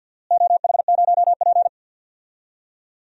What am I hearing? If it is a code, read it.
Morse code, heard as a keyed tone, about 48 wpm, 700 Hz. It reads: OH0P
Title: DXCC-100 Call Signs - 4 Characters - Repeated 2x - 50wpm